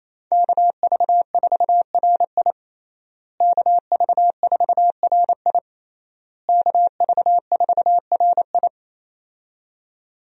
Morse code, heard as a keyed tone, about 28 wpm, 700 Hz. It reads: KV4RI KV4RI KV4RI
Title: USA Call Signs - 5 Characters - Repeated 3x - 28wpm